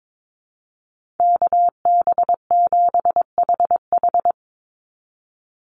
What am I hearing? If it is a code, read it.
Morse code, heard as a keyed tone, about 22 wpm, 700 Hz. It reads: KB7HH
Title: NA SOTA Chaser Callsigns 22wpm